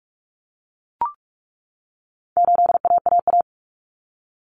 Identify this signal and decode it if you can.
Morse code, heard as a keyed tone, about 44 wpm, 700 Hz. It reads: 8AAA
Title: Common Cut Numbers - 4-Digit 45wpm